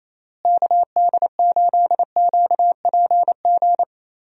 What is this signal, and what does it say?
Morse code, heard as a keyed tone, about 27 wpm, 700 Hz. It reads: KD8QPG
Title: USA Call Signs - 6 Characters - Repeated 3x - 28wpm